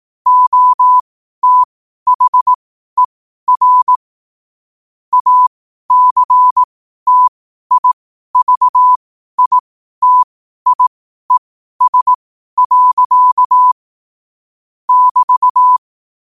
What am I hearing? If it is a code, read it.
Morse code, heard as a keyed tone, about 18 wpm, spaced as for 12 wpm, 1000 Hz. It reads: OTHER ACTIVITIES. <BT>
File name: qotdfiles / 12wpm